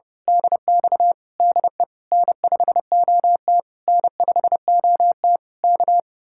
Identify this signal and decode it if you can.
Morse code, heard as a keyed tone, about 28 wpm, 700 Hz. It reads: DX DE N5OT N5OT K
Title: Speed-Racing Common QSO Phrases - 30wpm